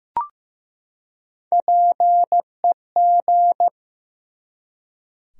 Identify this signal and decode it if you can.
Morse code, heard as a keyed tone, about 15 wpm, 700 Hz. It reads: PEG
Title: USA Common Names - Max 5 Letters - 15wpm